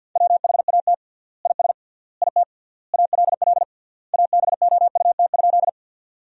Morse code, heard as a keyed tone, about 50 wpm, 700 Hz. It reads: WHAT IS IT ALL ABOUT?
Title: 3x Word-Spacing - Sentences from Top 100 Words - 50wpm